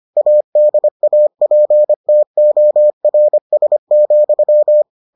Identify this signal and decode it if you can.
Morse code, heard as a keyed tone, about 25 wpm, 600 Hz. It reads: ADAPTORS,